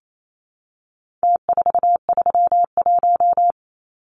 Morse code, heard as a keyed tone, about 28 wpm, 700 Hz. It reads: T431
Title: Numbers - Contest Style - 4-Digit 28wpm